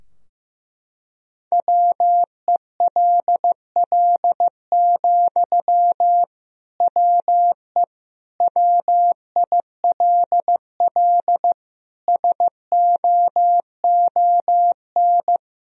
Morse code, heard as a keyed tone, about 15 wpm, 700 Hz. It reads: WELL, WE WILL SOON
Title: Sentences from Top 1000 Words - Review Part 4 - 15wpm